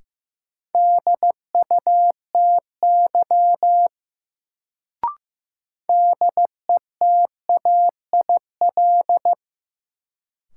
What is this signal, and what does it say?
Morse code, heard as a keyed tone, about 15 wpm, 700 Hz. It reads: DUTY DETAIL
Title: Sentences from Top 1500 Words - New Word Drill - 15wpm